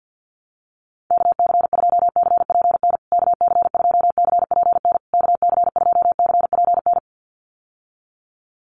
Morse code, heard as a keyed tone, about 51 wpm, 700 Hz. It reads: KC2CPN KC2CPN KC2CPN
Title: USA Call Signs Repeated 3x 50wpm